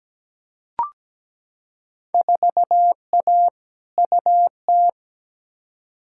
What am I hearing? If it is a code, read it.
Morse code, heard as a keyed tone, about 17 wpm, 700 Hz. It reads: 4A UT
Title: Field Day Exchanges - 17wpm